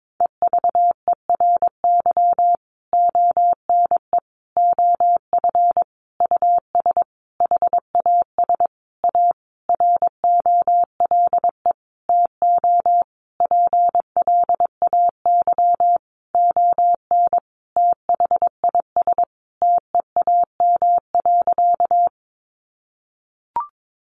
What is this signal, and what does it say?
Morse code, heard as a keyed tone, about 22 wpm, 700 Hz. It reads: EVERY ONE OF US HAS A ROLE TO PLAY ON THIS TEAM.
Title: Sentences from Top 1500 Words 22wpm